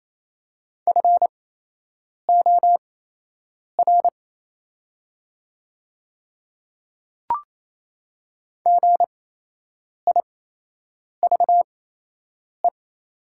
Morse code, heard as a keyed tone, about 28 wpm, spaced as for 8 wpm, 700 Hz. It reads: FOR GIVE
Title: 8x Character Spacing - Top 300 Words - 28wpm